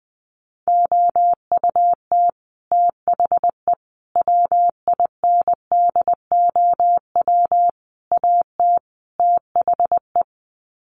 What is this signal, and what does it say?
Morse code, heard as a keyed tone, about 20 wpm, 700 Hz. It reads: OUT THE WINDOW AT THE
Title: Sets of 5 Words Encore2 20wpm